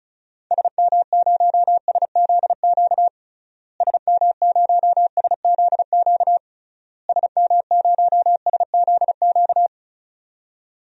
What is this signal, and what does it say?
Morse code, heard as a keyed tone, about 34 wpm, 700 Hz. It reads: SM0SZQ SM0SZQ SM0SZQ
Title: DXCC-100 Call Signs - 6 Characters - Repeated 3x - 35wpm